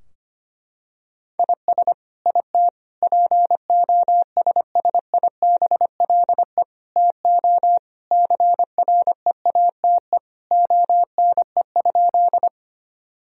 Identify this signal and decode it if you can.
Morse code, heard as a keyed tone, about 25 wpm, 700 Hz. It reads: IS IT POSSIBLE TO CREATE ONE?
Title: Sentences from Top 1000 Words - Review Part 3 - 25wpm